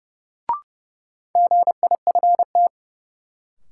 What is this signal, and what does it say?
Morse code, heard as a keyed tone, about 30 wpm, 700 Hz. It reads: GIFT
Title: Sentences from Top 900 Words - New Word Drill - 30wpm